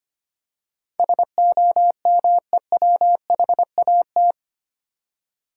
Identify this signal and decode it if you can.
Morse code, heard as a keyed tone, about 25 wpm, 700 Hz. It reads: SOMEWHAT
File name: Eight Letter Words - 25wpm